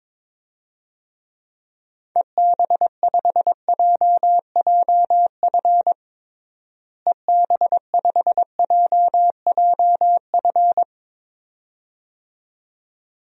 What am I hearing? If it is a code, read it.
Morse code, heard as a keyed tone, about 22 wpm, 700 Hz. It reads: EB5JJF EB5JJF
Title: DXCC-100 Call Signs - 6 Characters - Repeated 2x - 22wpm